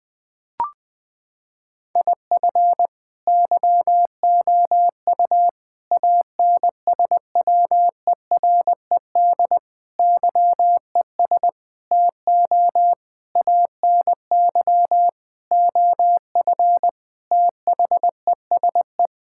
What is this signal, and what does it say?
Morse code, heard as a keyed tone, about 20 wpm, 700 Hz. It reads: IF YOU ANSWERED YES TO ANY OF THESE
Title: Sentences with 9-Word Phrases - 20wpm